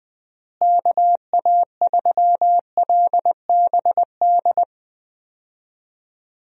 Morse code, heard as a keyed tone, about 21 wpm, 700 Hz. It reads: KA3LBD
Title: USA Call Signs - 6 Characters - Repeated 2x - 20wpm